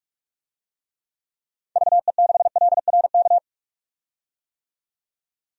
Morse code, heard as a keyed tone, about 42 wpm, 700 Hz. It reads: VE6LRK
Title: DXCC-100 Call Signs - 6 Characters - 45wpm